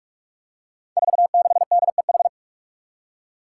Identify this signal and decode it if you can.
Morse code, heard as a keyed tone, about 44 wpm, 700 Hz. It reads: 46DEH
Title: Five Challenging Letter-Number Combinations 45wpm